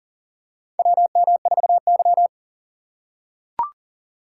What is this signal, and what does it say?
Morse code, heard as a keyed tone, about 40 wpm, 700 Hz. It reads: WM4Y
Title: USA Call Signs - 3x Character Spacing - 40wpm